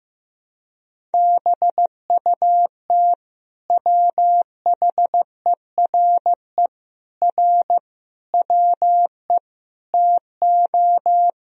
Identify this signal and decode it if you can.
Morse code, heard as a keyed tone, about 15 wpm, 700 Hz. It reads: BUT WHERE R WE TO
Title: Sentences from Top 200 Words and Abbreviations 15wpm